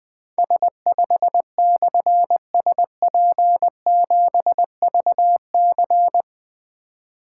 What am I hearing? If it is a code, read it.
Morse code, heard as a keyed tone, about 20 wpm, 700 Hz. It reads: S5/SP7VC
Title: Difficult Call Signs 20wpm